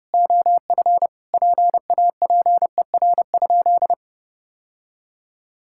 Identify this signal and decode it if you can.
Morse code, heard as a keyed tone, about 30 wpm, 700 Hz. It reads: OF PAPER?
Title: Sentences from Top 1600 Words 30wpm